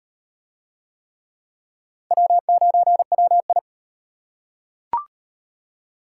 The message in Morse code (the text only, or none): W9WI